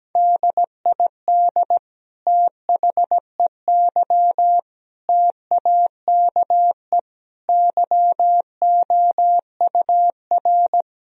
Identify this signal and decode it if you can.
Morse code, heard as a keyed tone, about 17 wpm, 700 Hz. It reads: DID THEY TAKE YOUR
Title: Sentences from Top 1000 Words - Review Part 3 - 17wpm